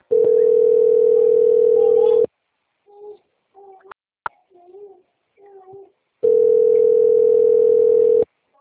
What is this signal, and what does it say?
Ringback tone (North America), 440 + 480 Hz.
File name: out- - - - - .34960